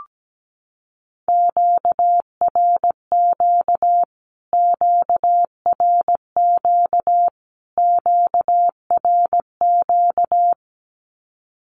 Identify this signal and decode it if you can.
Morse code, heard as a keyed tone, about 17 wpm, 700 Hz. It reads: QRQ QRQ QRQ
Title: Q-Codes Repeated 3x 17wpm